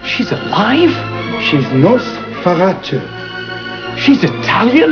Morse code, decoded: none